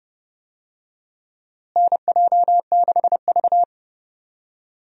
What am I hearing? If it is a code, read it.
Morse code, heard as a keyed tone, about 29 wpm, 700 Hz. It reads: NJ6V
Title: USA Call Signs - 4 Characters - Repeated 2x - 30wpm